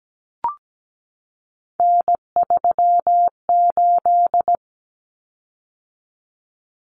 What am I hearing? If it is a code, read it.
Morse code, heard as a keyed tone, about 17 wpm, 700 Hz. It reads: N38
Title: Common Cut Numbers - 3-Digit 17wpm